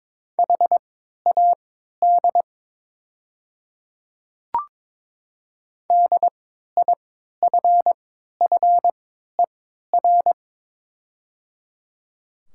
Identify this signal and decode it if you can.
Morse code, heard as a keyed tone, about 22 wpm, spaced as for 13 wpm, 700 Hz. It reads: HAD DIFFER
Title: 3x Character Spacing - Top 500 Words - 22wpm